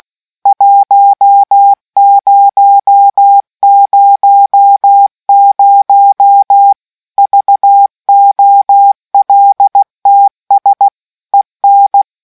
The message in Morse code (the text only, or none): 1000 VOLTS EN